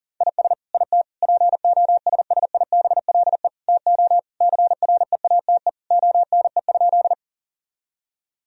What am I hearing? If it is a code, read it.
Morse code, heard as a keyed tone, about 37 wpm, 700 Hz. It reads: IS IT POSSIBLE TO CREATE ONE?
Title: Sentences from Top 1000 Words - Review Part 3 - 40wpm